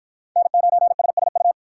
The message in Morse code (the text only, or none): N9SSU